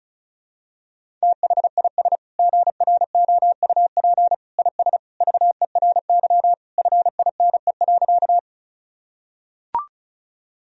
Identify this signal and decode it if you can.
Morse code, heard as a keyed tone, about 34 wpm, 700 Hz. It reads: THIS GROUP IS VERY FINE.
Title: Sentences from Top 1000 Words - Review Part 1 - 35wpm